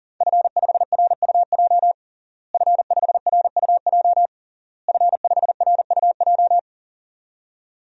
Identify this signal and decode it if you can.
Morse code, heard as a keyed tone, about 40 wpm, 700 Hz. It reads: F5RUJ F5RUJ F5RUJ
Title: DXCC-100 Call Signs - 5 Characters - Repeated 3x - 40wpm